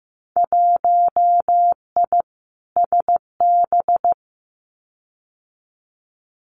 Morse code, heard as a keyed tone, about 15 wpm, 700 Hz. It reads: 1I SB
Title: Winter Field Day Exchanges 15wpm